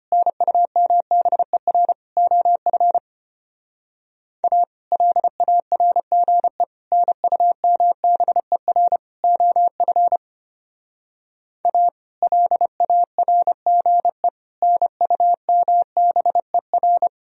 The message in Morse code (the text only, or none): NUMBER OF A LARGE NUMBER OF A LARGE NUMBER